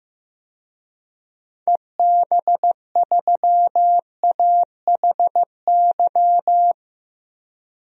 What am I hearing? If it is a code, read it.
Morse code, heard as a keyed tone, about 15 wpm, 700 Hz. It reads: EB3AHY
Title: DXCC-100 Call Signs - 6 Characters - 15wpm